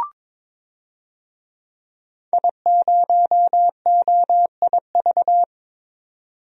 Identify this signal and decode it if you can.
Morse code, heard as a keyed tone, about 21 wpm, 700 Hz. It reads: I0OIV
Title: DXCC-100 Call Signs - 5 Characters - 22wpm